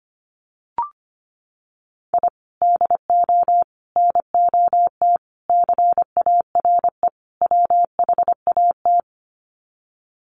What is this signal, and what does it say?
Morse code, heard as a keyed tone, about 25 wpm, 700 Hz. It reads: I DO NOT CARE WHAT
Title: Sets of 5 Words 25wpm